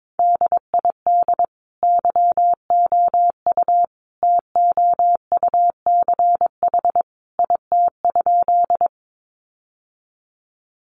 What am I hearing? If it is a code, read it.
Morse code, heard as a keyed tone, about 22 wpm, 700 Hz. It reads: DID YOU TOUCH IT?
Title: Sentences from Top 1200 Words 22wpm